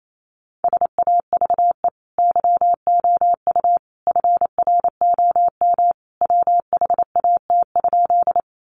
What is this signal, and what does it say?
Morse code, heard as a keyed tone, about 28 wpm, 700 Hz. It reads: SAVE YOU FROM WHAT?
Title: Sentences from Top 1100 Words 28wpm